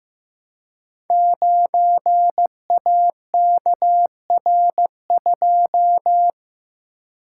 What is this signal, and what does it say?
Morse code, heard as a keyed tone, about 15 wpm, 700 Hz. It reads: 9AKR2
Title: Code Groups - 5 Letter:Numbers - 15wpm